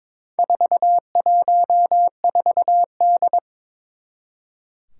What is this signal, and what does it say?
Morse code, heard as a keyed tone, about 22 wpm, 700 Hz. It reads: 414D
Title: Four Challenging Letter-Number Combinations 22wpm